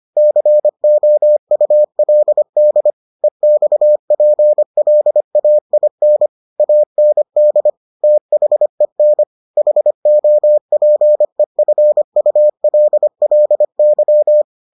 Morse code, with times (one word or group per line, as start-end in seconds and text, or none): 0.2-2.9 COULD
3.2-6.3 EXPLAIN
6.6-7.7 AND
8.0-9.2 THEN
9.6-14.4 HOPEFULLY